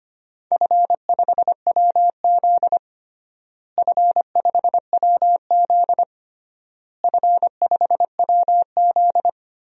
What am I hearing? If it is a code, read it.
Morse code, heard as a keyed tone, about 25 wpm, 700 Hz. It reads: F5WZ F5WZ F5WZ
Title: DXCC-100 Call Signs - 4 Characters - Repeated 3x - 25wpm